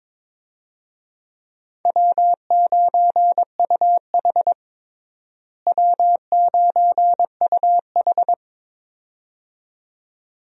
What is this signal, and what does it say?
Morse code, heard as a keyed tone, about 22 wpm, 700 Hz. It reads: W9UH W9UH
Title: USA Call Signs - 4 Characters - Repeated 2x - 22wpm